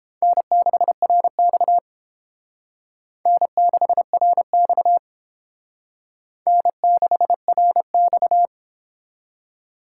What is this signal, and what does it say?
Morse code, heard as a keyed tone, about 29 wpm, 700 Hz. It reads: N6RX N6RX N6RX